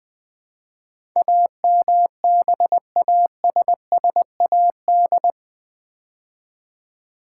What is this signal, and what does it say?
Morse code, heard as a keyed tone, about 20 wpm, 700 Hz. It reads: AMBASSAD
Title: Build A Word - 4 Letters At A Time - 10-Letter Words - 20wpm